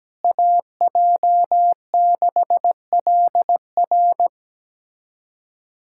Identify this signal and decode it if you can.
Morse code, heard as a keyed tone, about 17 wpm, 700 Hz. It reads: AJ6LR
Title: USA Callsign DE Callsign - 5 Characters - 17wpm